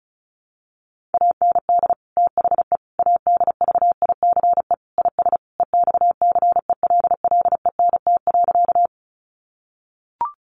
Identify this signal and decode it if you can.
Morse code, heard as a keyed tone, about 34 wpm, 700 Hz. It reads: AND THE ADVICE IS EXCELLENT.
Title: Sentences from Top 1100 Words 35wpm